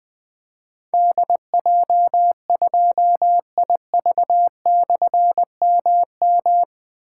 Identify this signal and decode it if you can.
Morse code, heard as a keyed tone, about 20 wpm, 700 Hz. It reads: DJ2IV/MM
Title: Difficult Call Signs 20wpm